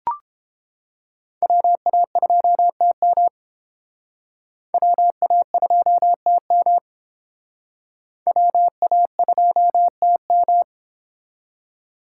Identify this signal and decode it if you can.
Morse code, heard as a keyed tone, about 30 wpm, 700 Hz. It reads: WA2TM WA2TM WA2TM